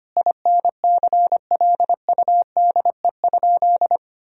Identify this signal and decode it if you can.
Morse code, heard as a keyed tone, about 25 wpm, 700 Hz. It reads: INCLUDE?
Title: Sentences from Top 1000 Words - Review Part 3 - 25wpm